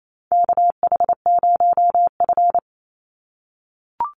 Morse code, heard as a keyed tone, about 28 wpm, 700 Hz. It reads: KH0F
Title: USA Call Signs 28wpm